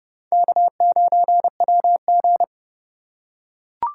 Morse code, heard as a keyed tone, about 30 wpm, 700 Hz. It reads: K9WG
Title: USA Call Signs - 4 Characters - 4x Character Spacing - 30wpm